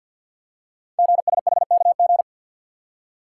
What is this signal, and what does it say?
Morse code, heard as a keyed tone, about 48 wpm, 700 Hz. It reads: GSHKG
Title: Code Groups - 5 Letters - 50wpm